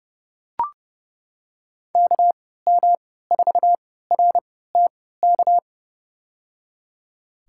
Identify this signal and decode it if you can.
Morse code, heard as a keyed tone, about 30 wpm, spaced as for 17 wpm, 700 Hz. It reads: KM4RTK
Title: USA Call Signs - 3x Character Spacing - 30wpm